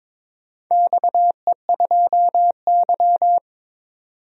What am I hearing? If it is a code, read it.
Morse code, heard as a keyed tone, about 22 wpm, 700 Hz. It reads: XE2Y
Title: DXCC-100 Call Signs - 4 Characters - 22wpm